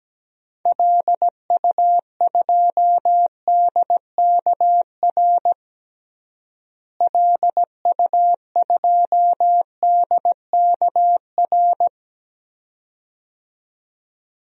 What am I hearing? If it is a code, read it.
Morse code, heard as a keyed tone, about 17 wpm, 700 Hz. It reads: LU2DKR LU2DKR
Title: DXCC-100 Call Signs - 6 Characters - Repeated 2x - 17wpm